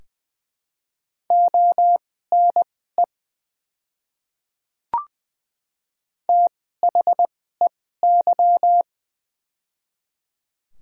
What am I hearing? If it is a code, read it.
Morse code, heard as a keyed tone, about 20 wpm, spaced as for 14 wpm, 700 Hz. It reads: ONE THEY
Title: 2x Character Spacing - Top 100 Words - 20wpm